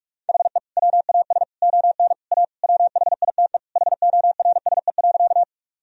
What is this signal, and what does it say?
Morse code, heard as a keyed tone, about 46 wpm, 700 Hz. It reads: HE WAS ON A WHITE HORSE.
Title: Sentences from Top 1000 Words - Review Part 1 - 45wpm